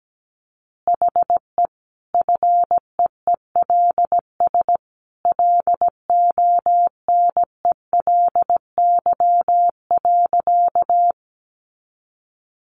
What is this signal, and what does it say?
Morse code, heard as a keyed tone, about 17 wpm, 700 Hz. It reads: HE FEELS LONELY.
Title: Easy Phrases 17wpm